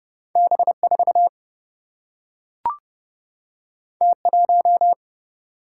Morse code, heard as a keyed tone, about 30 wpm, 700 Hz. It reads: B4 T1
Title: Single-Letter Single-Number 30wpm